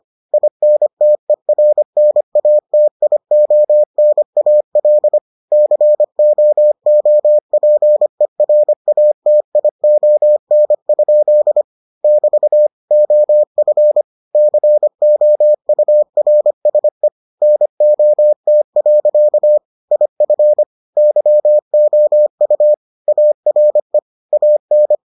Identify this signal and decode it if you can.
Morse code, heard as a keyed tone, about 25 wpm, 600 Hz. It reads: INTERNATIONAL COOPERATION? <BT> OF COURSE NOT. IF YOU ARE AN